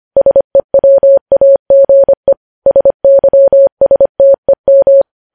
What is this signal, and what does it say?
Morse code, heard as a keyed tone, about 25 wpm, 550 Hz. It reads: SEWAGE SYSTEM
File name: podcast / News headlines-1x-25x25